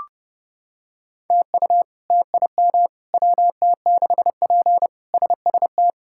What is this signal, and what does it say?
Morse code, heard as a keyed tone, about 32 wpm, 700 Hz. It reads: TU TIM WT6P SST
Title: K1USN SST QSO Practice - 30wpm